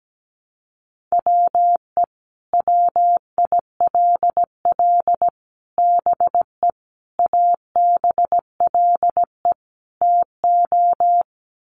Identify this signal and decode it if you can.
Morse code, heard as a keyed tone, about 17 wpm, 700 Hz. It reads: WE WILL BE ABLE TO
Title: Sets of 5 Words Encore 17wpm